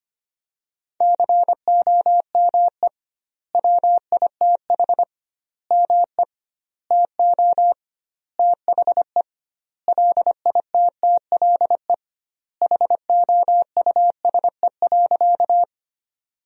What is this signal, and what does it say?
Morse code, heard as a keyed tone, about 25 wpm, 700 Hz. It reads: COME WITH ME TO THE LITTLE HOUSE.
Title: 2x Word-Spacing - Sentences from Top 200 Words - 25wpm